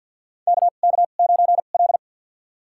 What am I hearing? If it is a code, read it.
Morse code, heard as a keyed tone, about 52 wpm, spaced as for 36 wpm, 700 Hz. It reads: KK9L
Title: USA Call Signs - 4 Characters - 2x Character Spacing - 50wpm